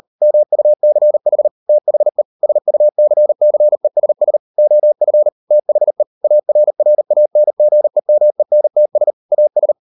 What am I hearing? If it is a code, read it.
Morse code, heard as a keyed tone, about 40 wpm, 600 Hz. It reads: MUCH THE SUCCESS OF THE ARRANGEMENTS AS